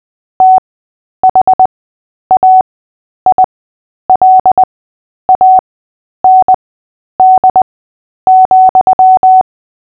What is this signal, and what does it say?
Morse code, heard as a keyed tone, about 20 wpm, spaced as for 10 wpm, 750 Hz. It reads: THAILAND,
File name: cw / fists / Short-Story-3-10wpm